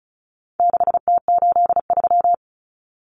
Morse code, heard as a keyed tone, about 34 wpm, 700 Hz. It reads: 6T83
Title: Common Cut Numbers - 4-Digit 35wpm